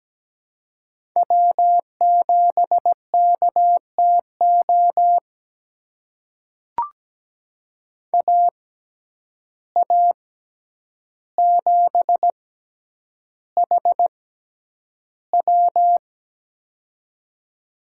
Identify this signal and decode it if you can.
Morse code, heard as a keyed tone, about 17 wpm, spaced as for 9 wpm, 700 Hz. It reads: W7KTO AA7HW
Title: USA Call Signs - 6x Character Spacing - 17wpm